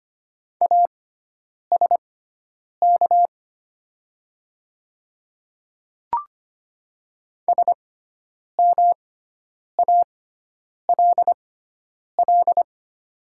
Morse code, heard as a keyed tone, about 25 wpm, spaced as for 9 wpm, 700 Hz. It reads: ASK SMALL
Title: 6x Character Spacing - Top 200 Words - 25wpm